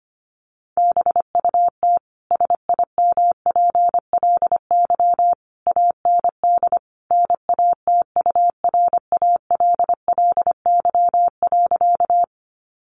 Morse code, heard as a keyed tone, about 25 wpm, 700 Hz. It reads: BUT SIMPLY AND NATURALLY.
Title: Sentences from Top 1300 Words 25wpm